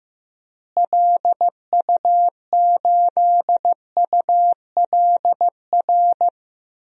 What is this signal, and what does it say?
Morse code, heard as a keyed tone, about 15 wpm, 700 Hz. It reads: LU8ULR
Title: DXCC-100 Call Signs - 6 Characters - Repeated 3x - 15wpm